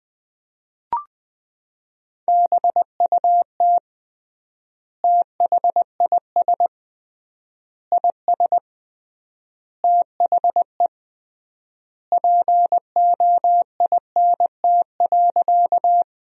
Morse code, heard as a keyed tone, about 20 wpm, 700 Hz. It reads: BUT THIS IS THE POINT.
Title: 3x Word-Spacing - Sentences from Top 200 Words - 20wpm